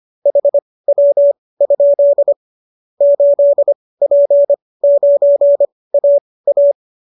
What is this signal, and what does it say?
Morse code, heard as a keyed tone, about 25 wpm, spaced as for 18 wpm, 570 Hz. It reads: HW? 8P9AA